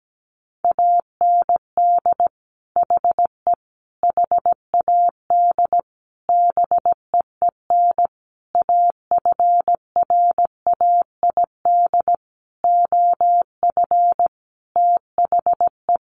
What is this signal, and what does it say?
Morse code, heard as a keyed tone, about 17 wpm, 700 Hz. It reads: AND HE HAD BEEN AFRAID OF THE
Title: Sentences from Top 1500 Words 17wpm